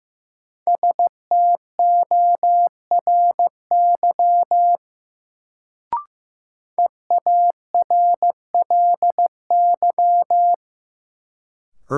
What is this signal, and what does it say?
Morse code, heard as a keyed tone, about 15 wpm, 700 Hz. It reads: STORY EARLY
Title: Sets of 3 Words - New Word Drill - 15wpm